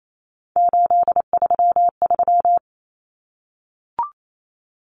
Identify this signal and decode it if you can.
Morse code, heard as a keyed tone, about 28 wpm, 700 Hz. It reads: 833
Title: Common Cut Numbers - 3-Digit 28wpm